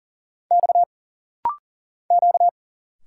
Morse code, heard as a keyed tone, about 40 wpm, 700 Hz. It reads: X Q
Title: Single Letters - Rapid-Fire 40wpm